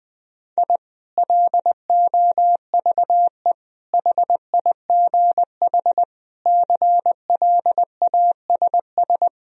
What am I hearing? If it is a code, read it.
Morse code, heard as a keyed tone, about 20 wpm, 700 Hz. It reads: I LOVE HIGH CLASS
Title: Sentences from Top 1000 Words - Review Part 4 - 20wpm